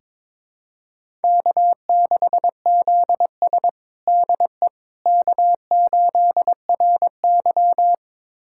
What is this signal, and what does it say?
Morse code, heard as a keyed tone, about 22 wpm, 700 Hz. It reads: K6ZS DE K8RY
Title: USA Callsign DE Callsign - 4 Characters - 22wpm